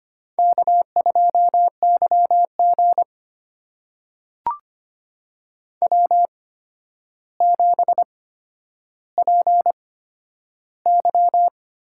Morse code, heard as a keyed tone, about 25 wpm, spaced as for 11 wpm, 700 Hz. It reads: K2YG W7PY
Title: USA Call Signs - 4 Characters - 8x Character Spacing - 25wpm